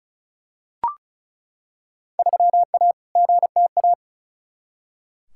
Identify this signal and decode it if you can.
Morse code, heard as a keyed tone, about 34 wpm, 700 Hz. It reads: 3A GTA
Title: Field Day Exchanges - 35wpm